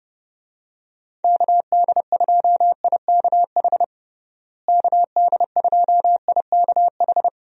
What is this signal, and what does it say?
Morse code, heard as a keyed tone, about 30 wpm, 700 Hz. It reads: KD2IKH KD2IKH
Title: USA Call Signs - 6 Characters - Repeated 2x - 30wpm